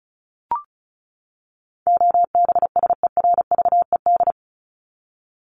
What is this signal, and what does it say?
Morse code, heard as a keyed tone, about 34 wpm, 700 Hz. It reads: OBSERVED
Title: ED Suffix - Top 100 Words with Base Words 35wpm